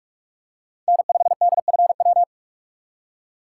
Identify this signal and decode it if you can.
Morse code, heard as a keyed tone, about 44 wpm, 700 Hz. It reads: N5DFW
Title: USA Call Signs - 5 Characters - 6x Character Spacing - 45wpm